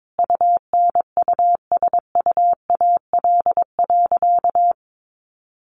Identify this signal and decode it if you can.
Morse code, heard as a keyed tone, about 22 wpm, 700 Hz. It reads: UNUSUAL.
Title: Sentences from Top 1400 Words 22wpm